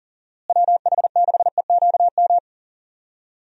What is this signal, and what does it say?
Morse code, heard as a keyed tone, about 37 wpm, 700 Hz. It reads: WH6EQM
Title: USA Call Signs - 6 Characters - Repeated 2x - 40wpm